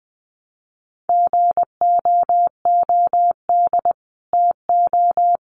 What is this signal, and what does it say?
Morse code, heard as a keyed tone, about 20 wpm, 700 Hz. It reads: GOOD TO
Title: Sets of 5 Words Encore2 20wpm